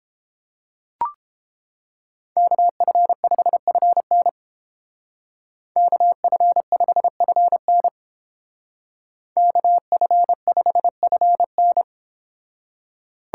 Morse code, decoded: KF5FN KF5FN KF5FN

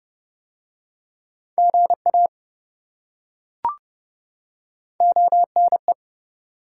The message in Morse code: GA ONE

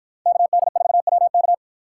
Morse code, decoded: KD4WK

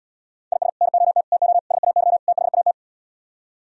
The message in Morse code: IS NOT TO SAY THAT

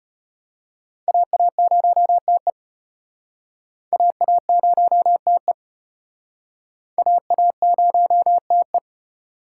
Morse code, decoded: AA0TE AA0TE AA0TE

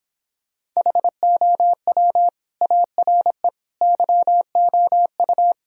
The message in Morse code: HOW ARE YOU

